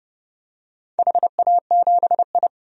Morse code, heard as a keyed tone, about 31 wpm, 700 Hz. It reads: HA7I
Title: DXCC-100 Call Signs - 4 Characters - 30wpm